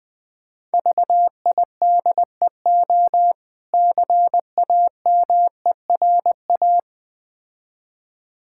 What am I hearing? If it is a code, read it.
Morse code, heard as a keyed tone, about 22 wpm, 690 Hz. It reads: VIDEO CAMERA
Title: Difficult Words - Speed-Racing with Hints - 20wpm